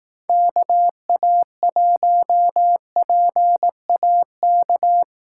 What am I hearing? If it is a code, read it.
Morse code, heard as a keyed tone, about 18 wpm, 700 Hz. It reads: KA1PAK
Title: Speed Racing - USA 2x3 Call Signs 15wpm